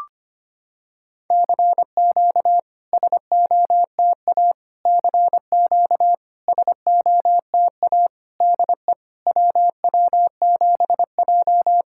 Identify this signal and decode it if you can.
Morse code, heard as a keyed tone, about 25 wpm, 700 Hz. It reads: CQ SOTA CQ SOTA DE WW7J
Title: SOTA QSO Practice 25wpm